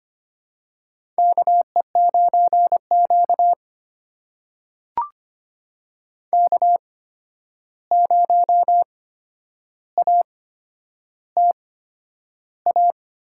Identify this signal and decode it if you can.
Morse code, heard as a keyed tone, about 25 wpm, spaced as for 10 wpm, 700 Hz. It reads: KE9Q K0ATA
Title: USA Call Signs - 8x Character Spacing - 25wpm